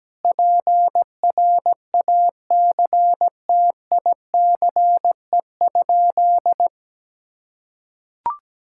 Morse code, heard as a keyed tone, about 17 wpm, 700 Hz. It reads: PRACTICE?
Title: Sentences from Top 1000 Words - Review Part 5 - 17wpm